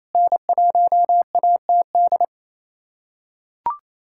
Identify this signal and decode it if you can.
Morse code, heard as a keyed tone, about 28 wpm, 700 Hz. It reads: N1ATD
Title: USA Callsign DE Callsign 28wpm